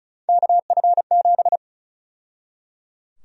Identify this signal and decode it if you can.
Morse code, heard as a keyed tone, about 34 wpm, 700 Hz. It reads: KF7
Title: USA Call Sign Prefixes - 35wpm